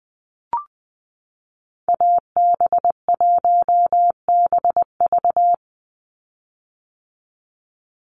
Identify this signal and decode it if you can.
Morse code, heard as a keyed tone, about 20 wpm, 700 Hz. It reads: AB1BV
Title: USA Call Signs 20wpm